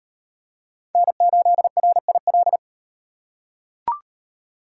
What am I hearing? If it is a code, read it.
Morse code, heard as a keyed tone, about 37 wpm, 700 Hz. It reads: N8RIL